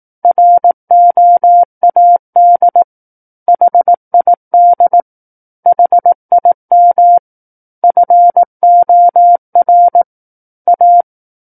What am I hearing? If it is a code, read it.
Morse code, heard as a keyed tone, about 18 wpm, 700 Hz. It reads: ROAD HID HIM FOR A
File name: 18 WPM